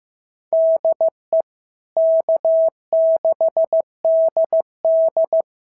Morse code, heard as a keyed tone, about 14 wpm, 650 Hz. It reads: DE K6DD
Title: SOTA QSO Practice 15wpm